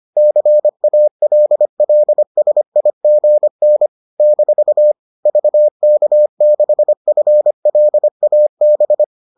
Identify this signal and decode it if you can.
Morse code, heard as a keyed tone, about 25 wpm, 600 Hz. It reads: CALLSIGN - VK6FLAB